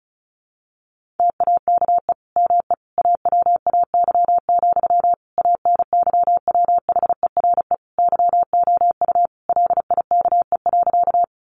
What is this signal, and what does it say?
Morse code, heard as a keyed tone, about 34 wpm, 700 Hz. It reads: TAKE ME AWAY, ANYWHERE YOU LIKE.
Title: Sentences from Top 1400 Words 35wpm